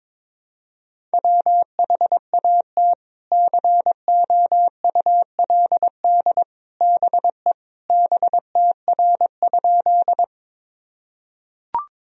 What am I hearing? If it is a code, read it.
Morse code, heard as a keyed tone, about 22 wpm, 700 Hz. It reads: WHAT COULD BE BTR?
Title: Sentences from Top 400 Words and Abbreviations 22wpm